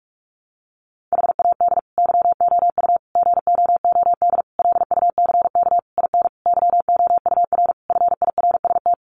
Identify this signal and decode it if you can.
Morse code, heard as a keyed tone, about 44 wpm, 700 Hz. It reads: HAD YOU GOOD LUCK IN YOUR FIRST
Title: Sentences from Top 1500 Words 45wpm